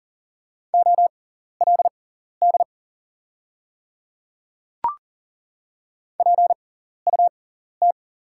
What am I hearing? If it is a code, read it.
Morse code, heard as a keyed tone, about 41 wpm, spaced as for 14 wpm, 700 Hz. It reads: OLD PUT
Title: 6x Character Spacing - Top 500 Words - 40wpm